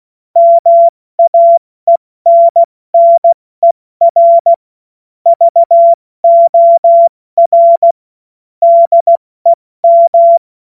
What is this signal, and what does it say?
Morse code, heard as a keyed tone, about 16 wpm, 680 Hz. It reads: MAENNER VOR DEM
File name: Witz-28-Oktober-2025-70-F80